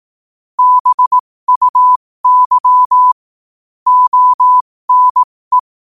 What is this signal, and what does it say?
Morse code, heard as a keyed tone, about 18 wpm, spaced as for 15 wpm, 1000 Hz. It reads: BUY ONE
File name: qotdfiles / 15wpm